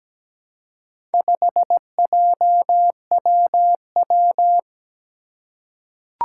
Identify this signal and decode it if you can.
Morse code, heard as a keyed tone, about 17 wpm, 700 Hz. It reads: 5JWW
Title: Four Challenging Letter-Number Combinations 17wpm